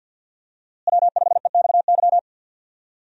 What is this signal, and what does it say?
Morse code, heard as a keyed tone, about 52 wpm, 700 Hz. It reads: W5EXY
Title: USA Call Signs - 5 Characters - Repeated 2x - 50wpm